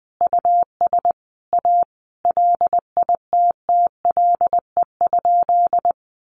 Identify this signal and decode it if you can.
Morse code, heard as a keyed tone, about 20 wpm, 700 Hz. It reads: US A LITTLE?
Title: Sentences from Top 900 Words 20wpm